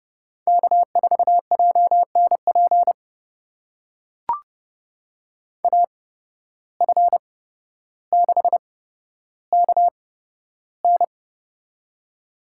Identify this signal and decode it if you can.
Morse code, heard as a keyed tone, about 32 wpm, spaced as for 13 wpm, 700 Hz. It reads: K4JNP AF6KN
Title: USA Call Signs - 5 Characters - 8x Character Spacing - 30wpm